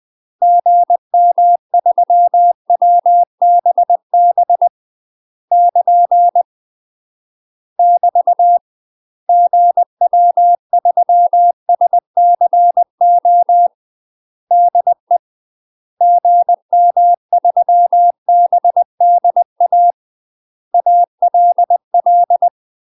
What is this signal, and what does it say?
Morse code, heard as a keyed tone, about 20 wpm, 700 Hz. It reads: GM3WBB <KN> <BT> GW3SCO DE GM3BDA ALL
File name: UK MASTER QSOs TRAINING / 20WPM / UK MASTER QSOs 0058